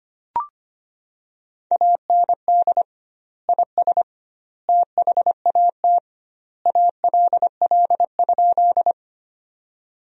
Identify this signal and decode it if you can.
Morse code, heard as a keyed tone, about 25 wpm, 700 Hz. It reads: AND IS THAT ALL?
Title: 2x Word-Spacing - Sentences from Top 100 Words - 25wpm